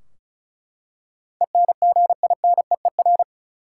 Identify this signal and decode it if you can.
Morse code, heard as a keyed tone, about 36 wpm, 700 Hz. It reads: ENGINEER
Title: Practice QSO - Exchange 3 - More Info - 35wpm